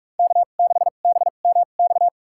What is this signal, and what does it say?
Morse code, heard as a keyed tone, about 42 wpm, spaced as for 31 wpm, 700 Hz. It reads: K6BMX
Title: USA Call Signs - 2x Character Spacing - 45wpm